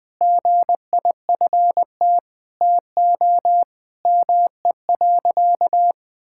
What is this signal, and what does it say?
Morse code, heard as a keyed tone, about 20 wpm, 700 Hz. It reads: GIFT TO ME.
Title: Sentences from Top 1000 Words - Review Part 6 - 20wpm